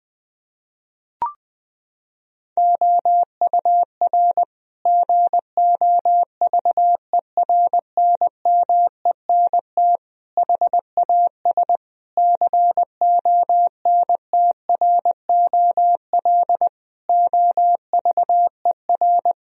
Morse code, heard as a keyed tone, about 20 wpm, 700 Hz. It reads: OUR GOVERNMENT HAS CONTROL OVER
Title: Sentences from Top 1000 Words - Review Part 4 - 20wpm